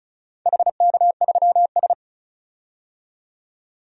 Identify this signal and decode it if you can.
Morse code, heard as a keyed tone, about 34 wpm, 700 Hz. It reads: HK3S
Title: DXCC-100 Call Signs - 4 Characters - 35wpm